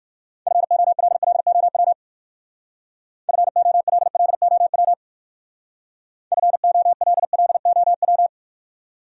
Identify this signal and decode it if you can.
Morse code, heard as a keyed tone, about 55 wpm, 700 Hz. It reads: FOLLOW FOLLOW FOLLOW
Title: Speed-Racing Top 200 Words 45wpm